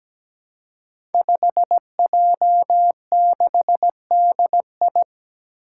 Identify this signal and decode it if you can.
Morse code, heard as a keyed tone, about 17 wpm, 700 Hz. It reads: 5J6DI
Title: Five Challenging Letter-Number Combinations 17wpm